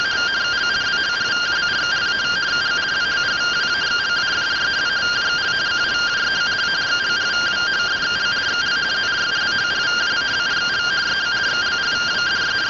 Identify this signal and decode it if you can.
Two-tone teleprinter signal, about 170 Hz shift.